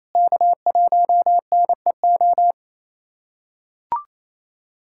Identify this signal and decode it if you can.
Morse code, heard as a keyed tone, about 28 wpm, 700 Hz. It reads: K1NEO
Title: USA Callsign DE Callsign - 5 Characters - 28wpm